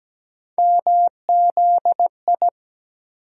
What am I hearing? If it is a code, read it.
Morse code, heard as a keyed tone, about 17 wpm, 700 Hz. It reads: MZI
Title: USA Call Sign Postfixes - 3 Letters 17wpm